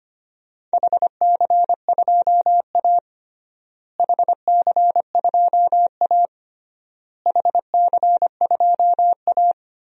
Morse code, heard as a keyed tone, about 25 wpm, 700 Hz. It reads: HC2A HC2A HC2A
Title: DXCC-100 Call Signs - 4 Characters - Repeated 3x - 25wpm